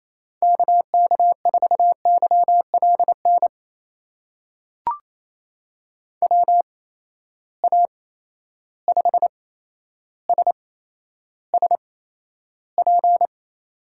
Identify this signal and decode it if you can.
Morse code, heard as a keyed tone, about 28 wpm, spaced as for 12 wpm, 700 Hz. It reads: KK4YLN WA5SSP
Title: USA Call Signs - 6 Characters - 8x Character Spacing - 28wpm